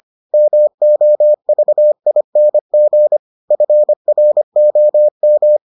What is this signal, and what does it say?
Morse code, heard as a keyed tone, about 25 wpm, 600 Hz. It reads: MOVING FROM